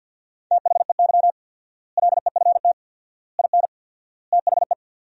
Morse code, heard as a keyed tone, about 51 wpm, 700 Hz. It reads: THEY LEFT IN THE
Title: 4x Word-Spacing - Sentences from Top 400 Words - 50wpm